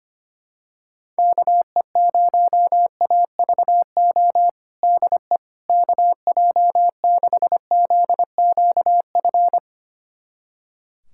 Morse code, heard as a keyed tone, about 25 wpm, 700 Hz. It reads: KE0AVO DE KJ6ZQF
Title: USA Callsign DE Callsign 25wpm